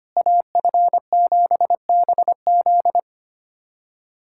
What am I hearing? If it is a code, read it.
Morse code, heard as a keyed tone, about 25 wpm, 700 Hz. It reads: AF7BZ
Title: USA Callsign DE Callsign - 5 Characters - 25wpm